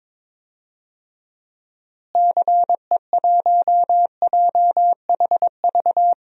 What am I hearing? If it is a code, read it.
Morse code, heard as a keyed tone, about 21 wpm, 700 Hz. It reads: CE1JHV
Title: DXCC-100 Call Signs - 6 Characters - Repeated 3x - 22wpm